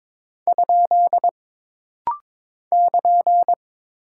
Morse code, heard as a keyed tone, about 22 wpm, 700 Hz. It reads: ? <KN>
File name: practice / 22wpm / Single Punctuation-Prosign - Rapid-Fire - 22wpm